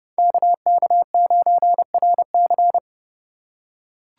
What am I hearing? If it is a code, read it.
Morse code, heard as a keyed tone, about 29 wpm, 700 Hz. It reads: KK9RC